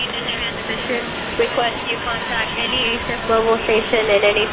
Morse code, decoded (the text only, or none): none